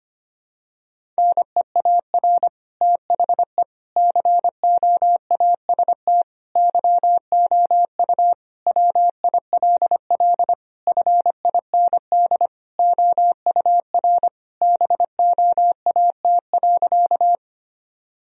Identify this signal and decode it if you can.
Morse code, heard as a keyed tone, about 25 wpm, 700 Hz. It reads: NEAR THE COAST YOU WILL FIND OUR BOAT.
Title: Sentences from Top 1000 Words - Review Part 4 - 25wpm